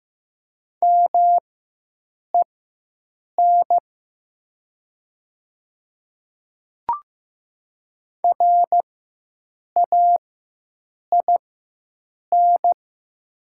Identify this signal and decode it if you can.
Morse code, heard as a keyed tone, about 15 wpm, spaced as for 7 wpm, 700 Hz. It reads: MEN RAIN
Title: 4x Character Spacing - Top 300 Words - 15wpm